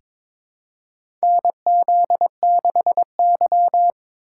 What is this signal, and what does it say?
Morse code, heard as a keyed tone, about 22 wpm, 700 Hz. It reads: NZ6Y
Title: USA Call Signs - 4 Characters - 3x Character Spacing - 22wpm